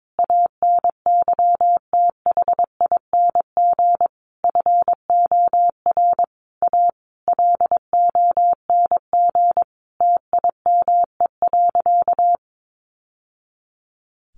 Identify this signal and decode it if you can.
Morse code, heard as a keyed tone, about 22 wpm, 700 Hz. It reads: ANYTHING FOR A LONG TIME.
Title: Sentences from Top 1000 Words 22wpm